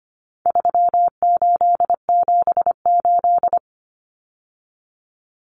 Morse code, heard as a keyed tone, about 25 wpm, 700 Hz. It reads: 3878
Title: Numbers - 4-Digit 25wpm